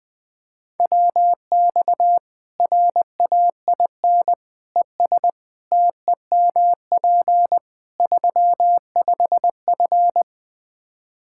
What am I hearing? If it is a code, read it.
Morse code, heard as a keyed tone, about 20 wpm, 700 Hz. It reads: WX RAIN ES TEMP 35F
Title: Practice QSO - Exchange 2 - Info - 20wpm